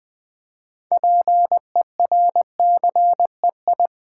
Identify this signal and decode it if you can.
Morse code, heard as a keyed tone, about 20 wpm, 700 Hz. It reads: PERCEI
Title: Build A Word - 2 Letters At A Time - 8-Letter Words - 20wpm